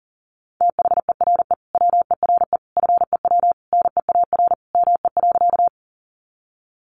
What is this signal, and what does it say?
Morse code, heard as a keyed tone, about 40 wpm, 700 Hz. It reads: THERE WERE FEW NEAR ME.
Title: Sentences from Top 300 Words 40wpm